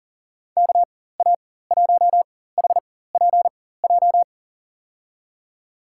Morse code, heard as a keyed tone, about 39 wpm, spaced as for 18 wpm, 700 Hz. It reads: KA1HPJ
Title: USA Call Signs - 6 Characters - 4x Character Spacing - 40wpm